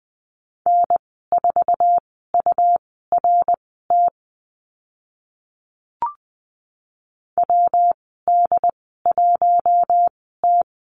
Morse code, heard as a keyed tone, about 20 wpm, spaced as for 14 wpm, 700 Hz. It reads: N4URT WD1T
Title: 140 Lesson 23g - Letter f - Call Signs 20wpm